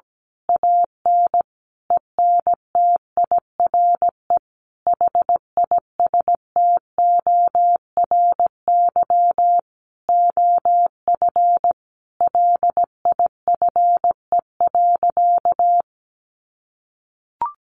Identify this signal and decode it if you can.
Morse code, heard as a keyed tone, about 17 wpm, 700 Hz. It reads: AN ENTIRE HISTORY OF LIFE.
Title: A Sound Of Thunder 17wpm